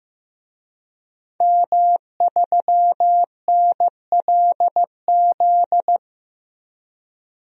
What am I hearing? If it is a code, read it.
Morse code, heard as a keyed tone, about 15 wpm, 700 Hz. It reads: M3NLZ
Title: DXCC-100 Call Signs - 5 Characters - 15wpm